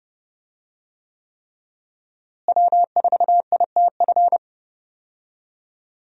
Morse code, heard as a keyed tone, about 30 wpm, 700 Hz. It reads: W4ITF
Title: USA Call Signs - 5 Characters - Repeated 3x - 30wpm